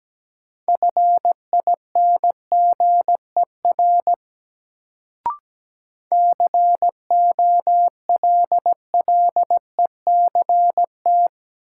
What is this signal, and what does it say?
Morse code, heard as a keyed tone, about 17 wpm, 700 Hz. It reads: FINGER COLLECT
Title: Sentences from Top 1600 Words - New Word Drill Alt - 17wpm